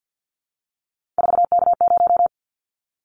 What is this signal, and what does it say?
Morse code, heard as a keyed tone, about 50 wpm, 700 Hz. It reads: 4K0
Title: Callsign Prefixes for Top 100 DXCC - 50wpm